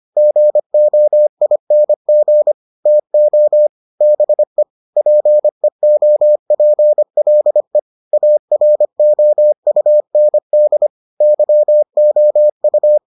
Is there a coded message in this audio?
Yes: Morse code from 0.2 to 13.0 s, heard as a keyed tone, about 25 wpm, 600 Hz. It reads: GOING TO BE PEOPLE AROUND YOU